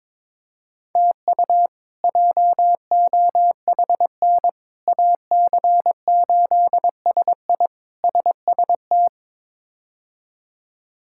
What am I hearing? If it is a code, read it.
Morse code, heard as a keyed tone, about 22 wpm, 700 Hz. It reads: TU JOHN AC8SI SST
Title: K1USN SST QSO Practice - 22wpm